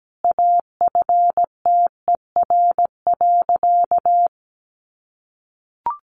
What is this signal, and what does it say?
Morse code, heard as a keyed tone, about 17 wpm, 700 Hz. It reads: AFTER.
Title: Sentences from Top 500 Words 17wpm